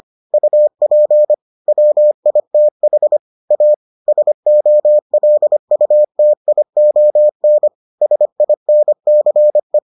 Morse code, heard as a keyed tone, about 25 wpm, 600 Hz. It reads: UP WITH A SOLUTION SINCE